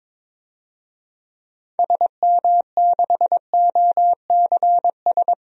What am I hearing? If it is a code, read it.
Morse code, heard as a keyed tone, about 21 wpm, 700 Hz. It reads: SM6OCS
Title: DXCC-100 Call Signs - 6 Characters - 22wpm